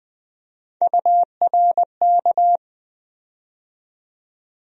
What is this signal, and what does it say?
Morse code, heard as a keyed tone, about 20 wpm, 700 Hz. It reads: URK
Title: USA Call Sign Postfixes - 3 Letters 20wpm